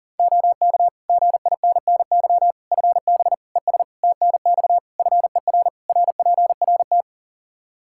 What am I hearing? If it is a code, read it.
Morse code, heard as a keyed tone, about 40 wpm, 700 Hz. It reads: OK GINNY FB ES TNX FER RPRT
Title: Practice QSO - Exchange 2 - Info - 40wpm